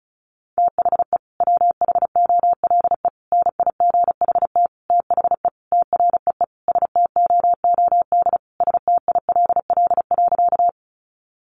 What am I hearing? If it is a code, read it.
Morse code, heard as a keyed tone, about 34 wpm, 700 Hz. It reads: THE WHOLE NIGHT THE TREE STOOD STILL.
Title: Sentences from Top 500 Words 35wpm